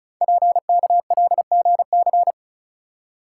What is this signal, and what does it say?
Morse code, heard as a keyed tone, about 36 wpm, 690 Hz. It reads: PKLGC
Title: Code Groups - 5 Letter:Numbers - 35wpm